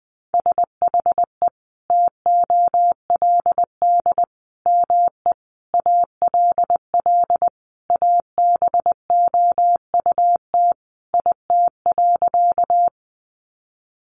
Morse code, heard as a keyed tone, about 20 wpm, 700 Hz. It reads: SHE TOLD ME ALL ABOUT IT.
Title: Sentences from Top 400 Words 20wpm